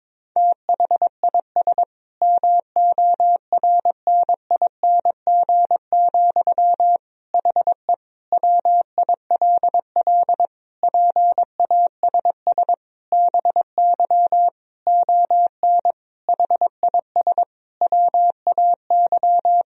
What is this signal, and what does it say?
Morse code, heard as a keyed tone, about 22 wpm, 700 Hz. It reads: THIS MORNING, HE WILL PASS BY ON HIS WAY
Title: Sentences from Top 1000 Words - Review Part 1 - 22wpm